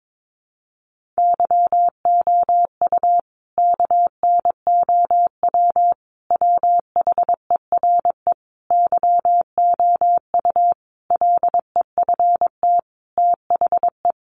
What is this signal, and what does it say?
Morse code, heard as a keyed tone, about 22 wpm, 700 Hz. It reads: YOU KNOW WHERE YOU LEFT THE
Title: Sentences from Top 1500 Words 22wpm